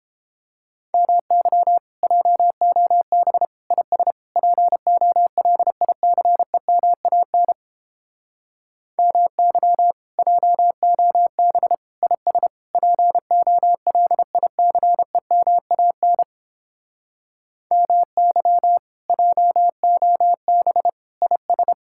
Morse code, heard as a keyed tone, about 29 wpm, 700 Hz. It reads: MY JOB IS POLICEMAN MY JOB IS POLICEMAN MY JOB IS